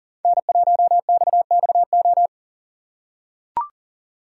Morse code, heard as a keyed tone, about 40 wpm, 700 Hz. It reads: N1XXO
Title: USA Call Signs - 5 Characters - 4x Character Spacing - 40wpm